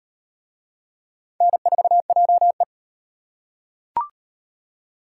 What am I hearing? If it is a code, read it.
Morse code, heard as a keyed tone, about 34 wpm, 700 Hz. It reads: N4JE